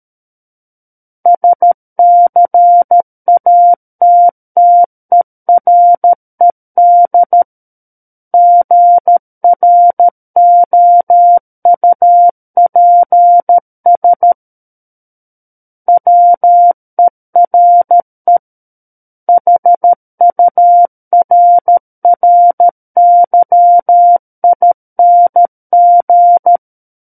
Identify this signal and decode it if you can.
Morse code, heard as a keyed tone, about 13 wpm, 700 Hz. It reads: SCATTERED GROUPS WERE HURRYING
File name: Book One / 04-Chapter 4 - 13 WPM